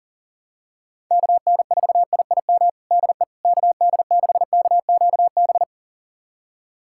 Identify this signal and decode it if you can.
Morse code, heard as a keyed tone, about 37 wpm, 700 Hz. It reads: KN4IIM DE KD6KQB
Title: USA Callsign DE Callsign 40wpm